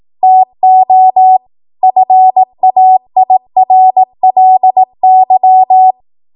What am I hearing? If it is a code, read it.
Morse code, heard as a keyed tone, about 18 wpm, 750 Hz. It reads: TO FAIRLY
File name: Morse / 18WPM